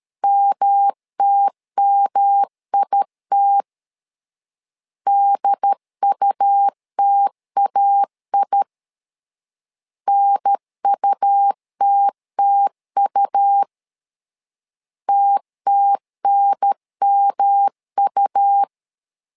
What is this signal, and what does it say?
Morse code, heard as a keyed tone, about 13 wpm, 800 Hz. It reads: MTMIT DUTAI NUTTU TTNMU
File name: LEZIONI / 3 MWRO / 3 M 12wpm 1541